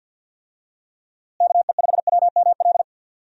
Morse code, heard as a keyed tone, about 46 wpm, 700 Hz. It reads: KE5WML
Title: USA Call Signs - 6 Characters - Repeated 2x - 50wpm